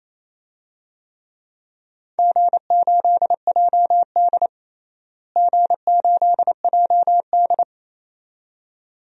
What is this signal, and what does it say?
Morse code, heard as a keyed tone, about 28 wpm, 700 Hz. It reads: G8JD G8JD
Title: DXCC-100 Call Signs - 4 Characters - Repeated 2x - 28wpm